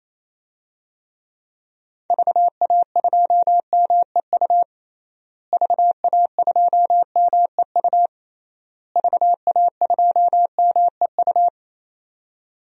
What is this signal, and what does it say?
Morse code, heard as a keyed tone, about 27 wpm, 700 Hz. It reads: VA2MEU VA2MEU VA2MEU
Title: DXCC-100 Call Signs - 6 Characters - Repeated 3x - 28wpm